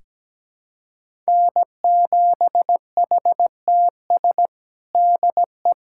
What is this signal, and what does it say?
Morse code, heard as a keyed tone, about 17 wpm, 700 Hz. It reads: N7HTS DE
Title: USA Callsign DE Callsign 17wpm